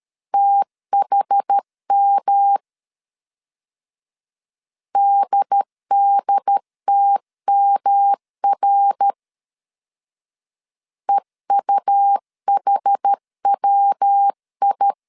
Morse code, heard as a keyed tone, about 13 wpm, 800 Hz. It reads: THM DDTMR EUHWI
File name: CW / DET / 1 3 1500C S A 12wpm 2905